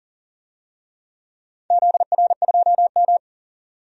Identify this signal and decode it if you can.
Morse code, heard as a keyed tone, about 40 wpm, 700 Hz. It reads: ZR2M
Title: DXCC-100 Call Signs - 4 Characters - 40wpm